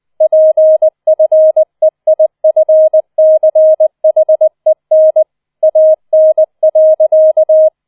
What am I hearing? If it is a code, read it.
Morse code, heard as a keyed tone, about 20 wpm, 630 Hz. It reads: PFEIFCHEN AN.